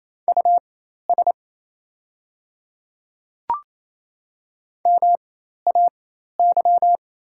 Morse code, heard as a keyed tone, about 28 wpm, spaced as for 13 wpm, 690 Hz. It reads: US MAY
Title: 4x Character Spacing - Top 200 Words - 28wpm